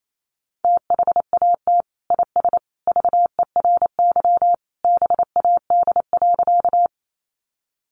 Morse code, heard as a keyed tone, about 28 wpm, 700 Hz. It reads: THAT IS VERY BAD.
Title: Sentences from Top 700 Words 28wpm